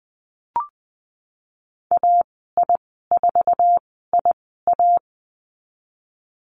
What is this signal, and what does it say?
Morse code, heard as a keyed tone, about 20 wpm, spaced as for 15 wpm, 700 Hz. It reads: AI4IA
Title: 048 Lesson 9g - Number 4 - Call Signs 20wpm